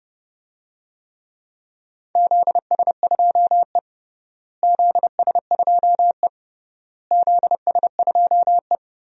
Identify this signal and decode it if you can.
Morse code, heard as a keyed tone, about 30 wpm, 700 Hz. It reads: ZS2E ZS2E ZS2E
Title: DXCC-100 Call Signs - 4 Characters - Repeated 3x - 30wpm